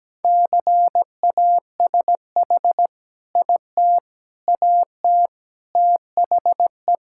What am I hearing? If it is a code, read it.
Morse code, heard as a keyed tone, about 17 wpm, 700 Hz. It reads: CASH IT AT THE
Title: Sentences from Top 1000 Words - Review Part 5 - 17wpm